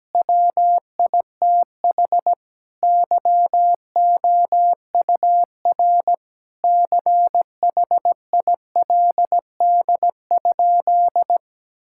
Morse code, heard as a keyed tone, about 17 wpm, 700 Hz. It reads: WITH YOUR CHILD?
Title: Sentences from Top 1000 Words - Review Part 6 - 17wpm